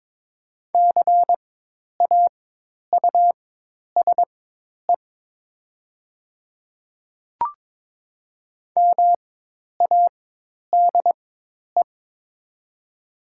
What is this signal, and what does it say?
Morse code, heard as a keyed tone, about 22 wpm, spaced as for 10 wpm, 700 Hz. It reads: CAUSE MADE
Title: 4x Character Spacing - Top 200 Words - 22wpm